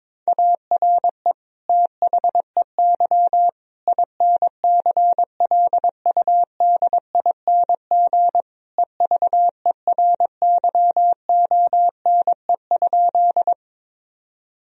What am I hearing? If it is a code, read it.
Morse code, heard as a keyed tone, about 22 wpm, 700 Hz. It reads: ARE THEY INCLUDING EVERYONE?
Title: Sentences from Top 1000 Words - Review Part 4 - 22wpm